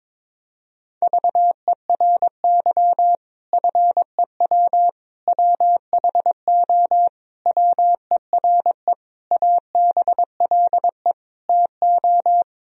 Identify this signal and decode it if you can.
Morse code, heard as a keyed tone, about 22 wpm, 700 Hz. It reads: VERY FEW WHO WERE ABLE TO
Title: Sentences from Top 1000 Words - Review Part 2 - 22wpm